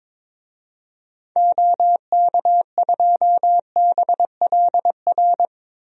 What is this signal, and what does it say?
Morse code, heard as a keyed tone, about 22 wpm, 700 Hz. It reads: OK2BLR
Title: DXCC-100 Call Signs - 6 Characters - Repeated 3x - 22wpm